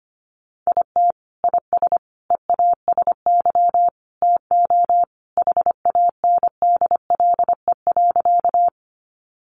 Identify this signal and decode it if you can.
Morse code, heard as a keyed tone, about 25 wpm, 700 Hz. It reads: IT IS EASY TO HANDLE.
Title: Sentences from Top 1100 Words 25wpm